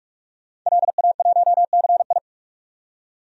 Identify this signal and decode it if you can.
Morse code, heard as a keyed tone, about 43 wpm, 700 Hz. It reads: LA1CI
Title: DXCC-100 Call Signs - 5 Characters - 45wpm